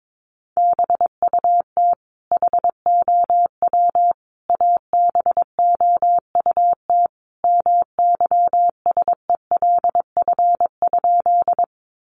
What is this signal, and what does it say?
Morse code, heard as a keyed tone, about 22 wpm, 700 Hz. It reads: BUT HOW ABOUT MYSELF?
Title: Sentences from Top 1300 Words 22wpm